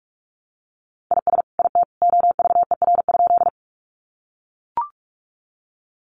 Morse code, heard as a keyed tone, about 44 wpm, 700 Hz. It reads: IS IT OVER?
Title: Sentences from Top 100 Words 45wpm